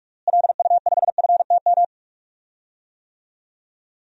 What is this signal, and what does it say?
Morse code, heard as a keyed tone, about 41 wpm, 700 Hz. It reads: LU5FTM
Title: DXCC-100 Call Signs - 6 Characters - Repeated 2x - 45wpm